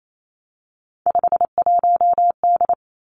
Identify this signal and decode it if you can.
Morse code, heard as a keyed tone, about 28 wpm, 700 Hz. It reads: 51D
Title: Three Challenging Letter-Number Combinations 28wpm